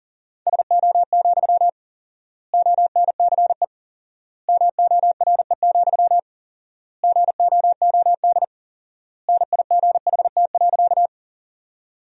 Morse code, heard as a keyed tone, about 37 wpm, 700 Hz. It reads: SO, ONCE MORE, GOOD NIGHT.
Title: 4x Word-Spacing - Sentences from Top 300 Words - 40wpm